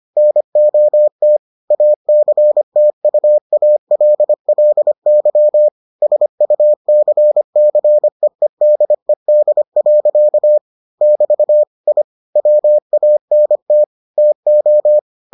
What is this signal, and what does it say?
Morse code, heard as a keyed tone, about 25 wpm, 600 Hz. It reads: NOT ACTUALLY SUCCEEDED. <BT> I WANT TO